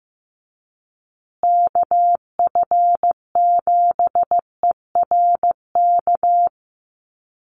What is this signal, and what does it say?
Morse code, heard as a keyed tone, about 15 wpm, 700 Hz. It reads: KF7ERK
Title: USA Call Signs Repeated 3x 15wpm